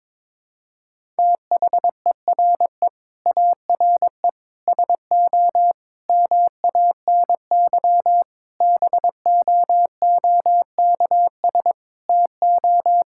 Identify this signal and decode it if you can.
Morse code, heard as a keyed tone, about 22 wpm, 700 Hz. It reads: THERE ARE SO MANY BOOKS TO
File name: practice / 22wpm / Sentences with 4-Word Phrases - 22wpm